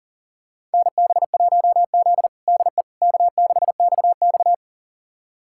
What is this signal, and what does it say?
Morse code, heard as a keyed tone, about 40 wpm, 700 Hz. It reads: NB1Z DE K6XX
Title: USA Callsign DE Callsign - 4 Characters - 40wpm